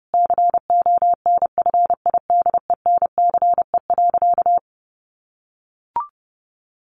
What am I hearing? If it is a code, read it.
Morse code, heard as a keyed tone, about 30 wpm, 700 Hz. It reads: CONFIDENCE.
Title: Sentences from Top 1500 Words 30wpm